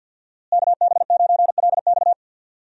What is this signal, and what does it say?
Morse code, heard as a keyed tone, about 51 wpm, 700 Hz. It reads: KB9LX
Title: USA Call Signs - 5 Characters - Repeated 2x - 50wpm